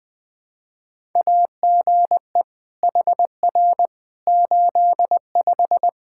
Morse code, heard as a keyed tone, about 20 wpm, 700 Hz. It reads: AGE HR 85
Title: Practice QSO - Exchange 3 - More Info - 20wpm